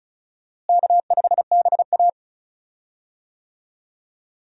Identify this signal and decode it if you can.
Morse code, heard as a keyed tone, about 34 wpm, 700 Hz. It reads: K5BA
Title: USA Call Signs - 4 Characters - Repeated 2x - 35wpm